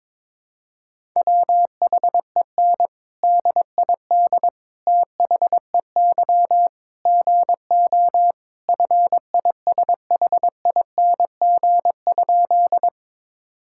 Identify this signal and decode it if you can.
Morse code, heard as a keyed tone, about 22 wpm, 700 Hz. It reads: WHEN DID THEY GO FISHING?
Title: Sentences from Top 1000 Words - Review Part 5 - 22wpm